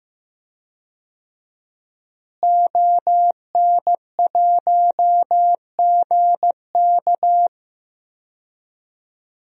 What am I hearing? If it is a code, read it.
Morse code, heard as a keyed tone, about 15 wpm, 700 Hz. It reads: ON1GK
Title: DXCC-100 Call Signs - 5 Characters - Repeated 2x - 15wpm